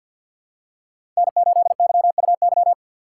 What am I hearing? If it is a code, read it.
Morse code, heard as a keyed tone, about 51 wpm, 700 Hz. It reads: N8YUY
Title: USA Call Signs - 5 Characters - 2x Character Spacing - 50wpm